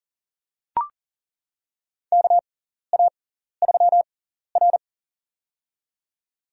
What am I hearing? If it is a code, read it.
Morse code, heard as a keyed tone, about 40 wpm, spaced as for 14 wpm, 700 Hz. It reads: KA3R
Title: USA Call Signs - 6x Character Spacing - 40wpm